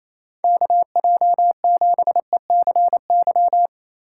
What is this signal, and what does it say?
Morse code, heard as a keyed tone, about 28 wpm, 700 Hz. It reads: KJ7ECY
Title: USA Call Signs - 6 Characters - 3x Character Spacing - 28wpm